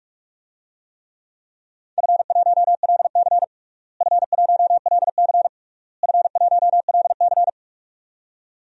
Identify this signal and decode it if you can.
Morse code, heard as a keyed tone, about 46 wpm, 700 Hz. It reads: F1LC F1LC F1LC
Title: DXCC-100 Call Signs - 4 Characters - Repeated 3x - 45wpm